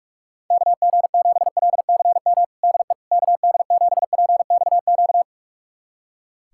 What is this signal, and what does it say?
Morse code, heard as a keyed tone, about 46 wpm, 700 Hz. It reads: KG7LCM DE KD7PXQ
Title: USA Callsign DE Callsign - 6 Characters - 45wpm